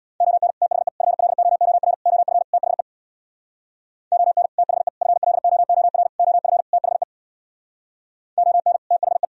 Morse code, heard as a keyed tone, about 64 wpm, 700 Hz. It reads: ON THE FLOOR OF THE ON THE FLOOR OF THE ON THE